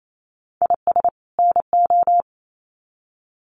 Morse code, heard as a keyed tone, about 28 wpm, 700 Hz. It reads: IS NO
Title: Sets of 2 Words 28wpm